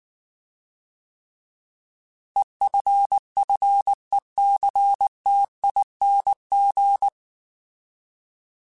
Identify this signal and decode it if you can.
Morse code, heard as a keyed tone, about 19 wpm, 800 Hz. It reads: EFFECTING